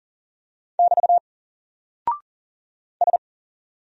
Morse code, heard as a keyed tone, about 40 wpm, 700 Hz. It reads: <BT> S
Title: Single Character (Letter:Number:Punctuation:Prosign) 40wpm